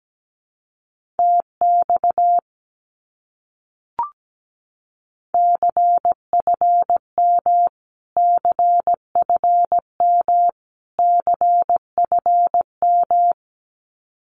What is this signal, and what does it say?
Morse code, heard as a keyed tone, about 17 wpm, 700 Hz. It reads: TX CFM CFM CFM
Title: Abbreviations Repeated 3x - 17wpm